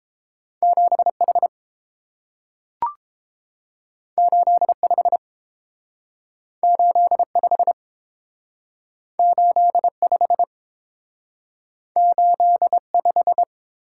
Morse code, heard as a keyed tone, about 30 wpm, 700 Hz. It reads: 7H 85 85 85 85